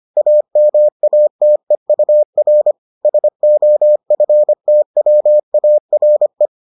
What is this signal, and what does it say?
Morse code, heard as a keyed tone, about 25 wpm, 600 Hz. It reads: AMATEUR SOFTWARE